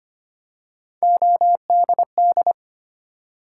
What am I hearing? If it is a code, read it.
Morse code, heard as a keyed tone, about 25 wpm, 700 Hz. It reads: ODD
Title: Three Letter Words - 25wpm